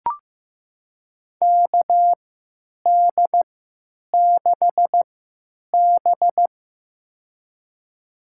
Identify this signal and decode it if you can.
Morse code, heard as a keyed tone, about 15 wpm, spaced as for 9 wpm, 700 Hz. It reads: KD6B
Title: USA Call Signs - 3x Character Spacing - 15wpm